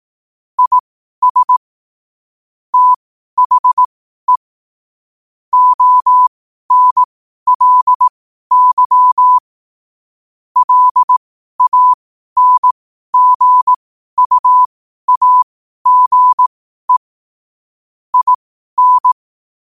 Morse code, heard as a keyed tone, about 18 wpm, spaced as for 12 wpm, 1000 Hz. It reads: IS THE ONLY LANGUAGE IN
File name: qotdfiles / 12wpm